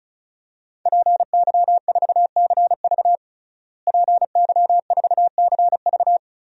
Morse code, heard as a keyed tone, about 36 wpm, 700 Hz. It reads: PY4CV PY4CV
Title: DXCC-100 Call Signs - 5 Characters - Repeated 2x - 35wpm